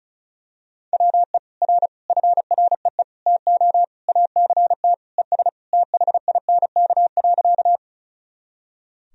Morse code, heard as a keyed tone, about 36 wpm, 700 Hz. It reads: WE R FREE TO ACT ES THINK.
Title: Sentences from Top 500 Words and Abbreviations 35wpm